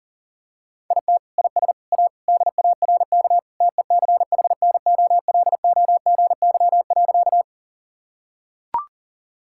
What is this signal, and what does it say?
Morse code, heard as a keyed tone, about 40 wpm, 700 Hz. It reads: IT IS A DARK TECHNOLOGY.
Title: Sentences from Top 1000 Words - Review Part 5 - 40wpm